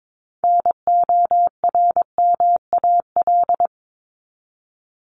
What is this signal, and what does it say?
Morse code, heard as a keyed tone, about 22 wpm, 700 Hz. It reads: NORMAL
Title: Top 1001-1250 Words 22wpm